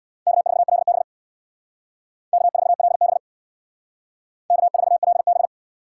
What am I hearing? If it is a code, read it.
Morse code, heard as a keyed tone, about 64 wpm, 690 Hz. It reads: K4LB K4LB K4LB